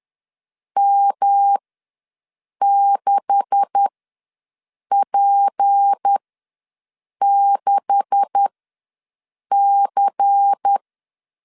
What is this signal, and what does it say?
Morse code, heard as a keyed tone, about 10 wpm, spaced as for 6 wpm, 800 Hz. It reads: M6P6C
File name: LEZIONI / 9 286 ws 10wpm 4032